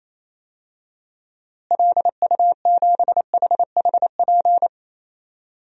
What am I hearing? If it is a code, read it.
Morse code, heard as a keyed tone, about 28 wpm, 700 Hz. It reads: LU7HHP
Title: DXCC-100 Call Signs - 6 Characters - Repeated 2x - 28wpm